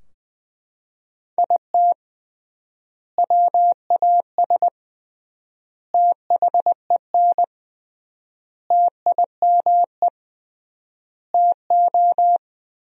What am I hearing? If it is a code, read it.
Morse code, heard as a keyed tone, about 19 wpm, 700 Hz. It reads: IT WAS THEN TIME TO
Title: 3x Word-Spacing - Sentences from Top 300 Words - 20wpm